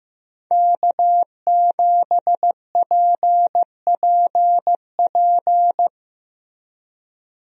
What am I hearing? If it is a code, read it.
Morse code, heard as a keyed tone, about 15 wpm, 700 Hz. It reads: K7PPP
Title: USA Call Signs - 5 Characters - Repeated 2x - 15wpm